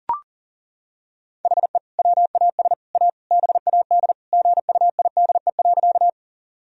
Morse code, heard as a keyed tone, about 37 wpm, 700 Hz. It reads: HE WAS A BAD GUIDE.
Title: Sentences from Top 1000 Words - Review Part 3 - 40wpm